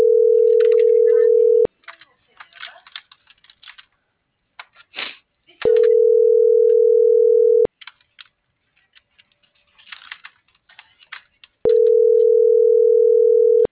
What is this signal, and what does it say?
Ringback tone (North America), 440 + 480 Hz.
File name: CENTRAL FRALSE / out aidee.cruz